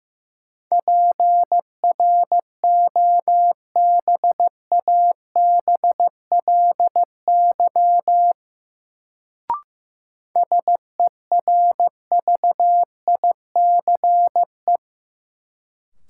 Morse code, heard as a keyed tone, about 15 wpm, 700 Hz. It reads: PROBABLY SERVICE
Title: Sentences from Top 700 Words - New Word Drill - 15wpm